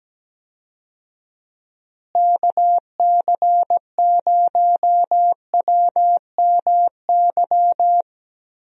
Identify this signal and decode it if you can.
Morse code, heard as a keyed tone, about 17 wpm, 700 Hz. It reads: KC0WMY
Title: USA Call Signs - 6 Characters - Repeated 2x - 17wpm